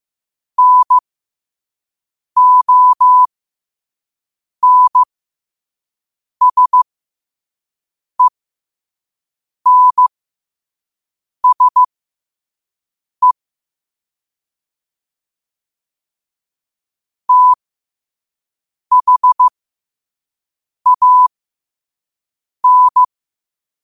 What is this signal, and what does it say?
Morse code, heard as a keyed tone, about 15 wpm, spaced as for 5 wpm, 1000 Hz. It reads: NONSENSE THAN
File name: 2026-03-16-5wpm